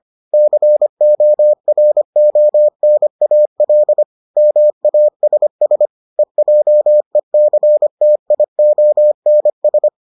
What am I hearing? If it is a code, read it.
Morse code, heard as a keyed tone, about 25 wpm, 600 Hz. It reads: CORONAL MASS EJECTIONS